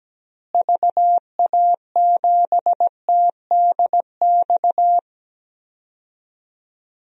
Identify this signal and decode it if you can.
Morse code, heard as a keyed tone, about 17 wpm, 700 Hz. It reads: VA7TDX
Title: DXCC-100 Call Signs - 6 Characters - Repeated 2x - 17wpm